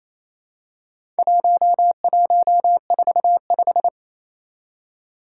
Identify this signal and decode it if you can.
Morse code, heard as a keyed tone, about 29 wpm, 700 Hz. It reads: 1145
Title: Four Challenging Letter-Number Combinations 28wpm